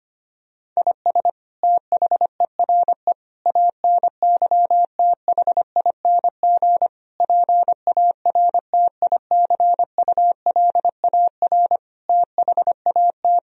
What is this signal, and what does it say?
Morse code, heard as a keyed tone, about 25 wpm, 700 Hz. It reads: IS THERE ANYTHING PARTICULAR THAT
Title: Sentences from Top 1000 Words - Review Part 4 - 25wpm